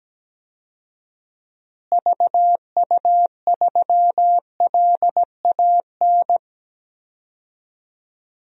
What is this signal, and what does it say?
Morse code, heard as a keyed tone, about 17 wpm, 700 Hz. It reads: VU3LAN
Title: DXCC-100 Call Signs - 6 Characters - Repeated 3x - 17wpm